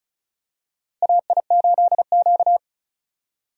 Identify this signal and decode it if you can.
Morse code, heard as a keyed tone, about 36 wpm, 700 Hz. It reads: AI8Q
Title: USA Call Signs - 4 Characters - Repeated 3x - 35wpm